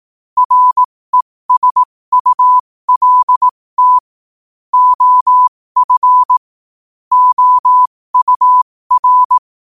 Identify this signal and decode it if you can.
Morse code, heard as a keyed tone, about 18 wpm, spaced as for 15 wpm, 1000 Hz. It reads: RESULT OF OUR
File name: qotdfiles / 15wpm